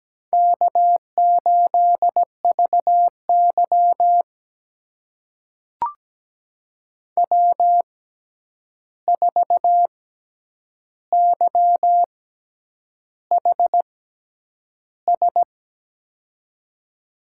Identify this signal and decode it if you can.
Morse code, heard as a keyed tone, about 17 wpm, spaced as for 8 wpm, 700 Hz. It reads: K8VY W4YHS
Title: USA Call Signs - 6x Character Spacing - 17wpm